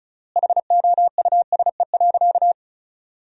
Morse code, heard as a keyed tone, about 36 wpm, 700 Hz. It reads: HOUSE.
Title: 3x Word-Spacing - Sentences from Top 300 Words - 35wpm